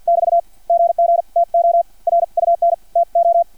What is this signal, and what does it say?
Morse code, heard as a keyed tone, about 50 wpm, spaced as for 44 wpm, 680 Hz. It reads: <BT> GG TO RUN TO
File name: archived / CRSnet